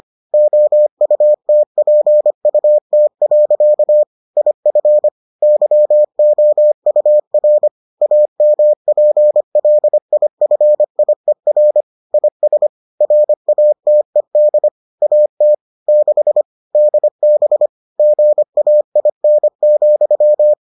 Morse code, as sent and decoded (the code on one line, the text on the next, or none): --- ..- - .--. ..- - .-.-.- / .. ..-. / -.-- --- ..- .-. / .- -- .--. .-.. .. ..-. .. .-. / .. ... / .-. .- - . -.. / .- - / -.... / -.. -... / --. .- .. -. --..--
OUTPUT. IF YOUR AMPLIFIER IS RATED AT 6 DB GAIN,